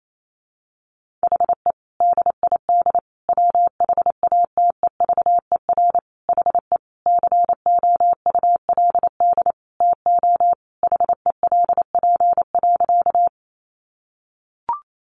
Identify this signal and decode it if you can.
Morse code, heard as a keyed tone, about 28 wpm, 700 Hz. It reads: HE DID WHATEVER HE COULD TO HELP.
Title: Sentences from Top 1100 Words 28wpm